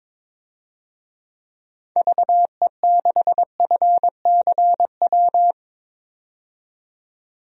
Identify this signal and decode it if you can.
Morse code, heard as a keyed tone, about 22 wpm, 700 Hz. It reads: VE6FCW
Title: DXCC-100 Call Signs - 6 Characters - 22wpm